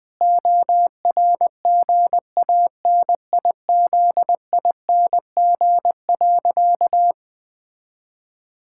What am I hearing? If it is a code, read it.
Morse code, heard as a keyed tone, about 19 wpm, 700 Hz. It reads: ORGANIZING.
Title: Sentences with 7-Word Phrases - Spoken Beginning - 20wpm